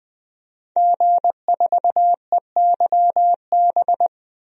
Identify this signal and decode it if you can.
Morse code, heard as a keyed tone, about 20 wpm, 700 Hz. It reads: G4EYB
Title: DXCC-100 Call Signs - 5 Characters - 20wpm